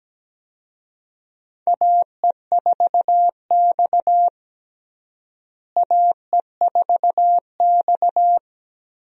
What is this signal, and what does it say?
Morse code, heard as a keyed tone, about 17 wpm, 700 Hz. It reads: AE4X AE4X
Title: USA Call Signs - 4 Characters - Repeated 3x - 17wpm